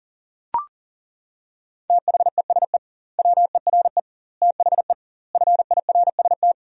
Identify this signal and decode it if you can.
Morse code, heard as a keyed tone, about 40 wpm, 700 Hz. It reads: THESE WERE THE FIRST
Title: 2x Word-Spacing - Sentences from Top 300 Words - 40wpm